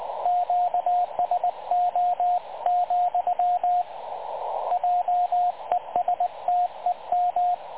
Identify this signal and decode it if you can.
Morse code, heard as a keyed tone, about 22 wpm, 750 Hz, with radio noise. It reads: QSO, JESTEM
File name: dlazaawansowanych / nagraniazpasma / websdr recording 3565.0kHz